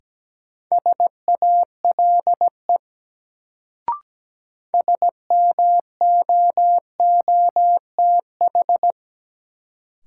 Morse code, heard as a keyed tone, about 17 wpm, 700 Hz. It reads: SALE SMOOTH
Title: Sentences from Top 1400 Words - New Word Drill Alt - 17wpm